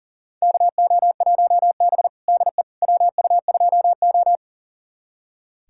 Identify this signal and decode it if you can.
Morse code, heard as a keyed tone, about 40 wpm, 690 Hz. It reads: KO1B DE WU2O
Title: USA Callsign DE Callsign - 4 Characters - 40wpm